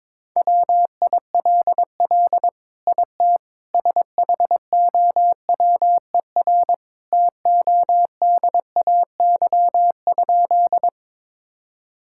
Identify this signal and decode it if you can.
Morse code, heard as a keyed tone, about 22 wpm, 700 Hz. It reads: WILL IT SHOWER TODAY?
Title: Sentences from Top 1600 Words 22wpm